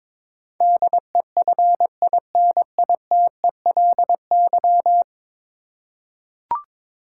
Morse code, heard as a keyed tone, about 22 wpm, 700 Hz. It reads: DEFINITELY
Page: morse-code-ninja-practice-22wpm